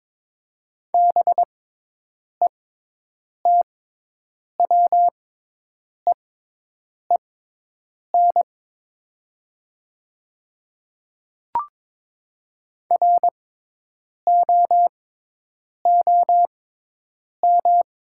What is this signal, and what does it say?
Morse code, heard as a keyed tone, about 22 wpm, spaced as for 8 wpm, 700 Hz. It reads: BETWEEN ROOM
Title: 6x Character Spacing - Top 300 Words - 22wpm